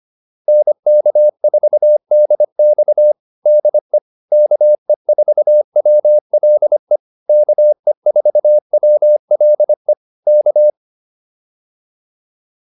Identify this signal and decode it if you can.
Morse code, heard as a keyed tone, about 25 wpm, 600 Hz. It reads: NK4DX DE KE4WLE KE4WLE K